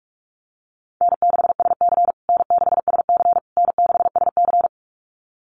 Morse code, heard as a keyed tone, about 44 wpm, 700 Hz. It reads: N6SC N6SC N6SC
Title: USA Call Signs Repeated 3x 45wpm